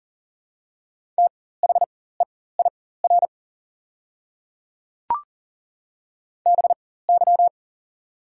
Morse code, heard as a keyed tone, about 40 wpm, spaced as for 19 wpm, 700 Hz. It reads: THEIR BY
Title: 4x Character Spacing - Top 200 Words - 40wpm